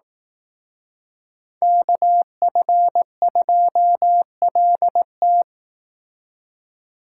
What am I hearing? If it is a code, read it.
Morse code, heard as a keyed tone, about 18 wpm, 700 Hz. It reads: KF2LT
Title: Speed Racing - USA 2x2 Call Signs 15wpm